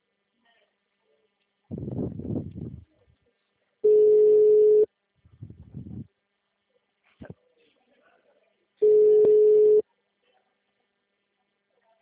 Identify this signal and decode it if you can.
Ringback tone (Europe, 425 Hz), with speech.